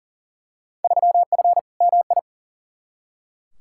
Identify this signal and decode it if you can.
Morse code, heard as a keyed tone, about 40 wpm, 700 Hz. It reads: 3F MI
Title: Field Day Exchanges - 40wpm